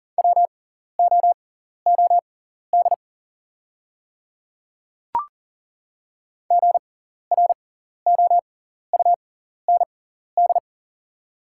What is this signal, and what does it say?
Morse code, heard as a keyed tone, about 38 wpm, spaced as for 14 wpm, 700 Hz. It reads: WOOD GROUND
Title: 6x Character Spacing - Top 500 Words - 40wpm